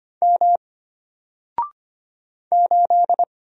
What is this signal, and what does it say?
Morse code, heard as a keyed tone, about 25 wpm, 700 Hz. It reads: M 8
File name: Single-Letter-Number-25wpm